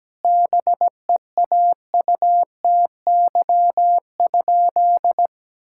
Morse code, heard as a keyed tone, about 17 wpm, 700 Hz. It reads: BEAUTY?
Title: Sentences from Top 1000 Words - Review Part 5 - 17wpm